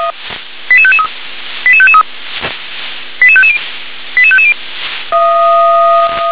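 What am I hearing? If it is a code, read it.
Five-tone selective call (ZVEI), heard as quick runs of five tones, tones of about 70 ms. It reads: 88553 88553 88550 88550
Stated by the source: ZVEI five-tone selective call received on an SDR.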